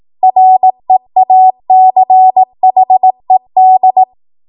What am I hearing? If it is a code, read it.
Morse code, heard as a keyed tone, about 18 wpm, 750 Hz. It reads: REACHED
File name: Archive / 18 WPM / 18WPM